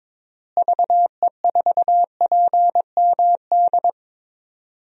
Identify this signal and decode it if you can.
Morse code, heard as a keyed tone, about 22 wpm, 700 Hz. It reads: VE4PMD
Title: DXCC-100 Call Signs - 6 Characters - Repeated 3x - 22wpm